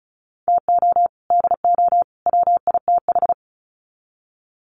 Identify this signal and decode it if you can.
Morse code, heard as a keyed tone, about 34 wpm, 700 Hz. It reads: TO DO WITH
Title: Sets of 3 Words 35wpm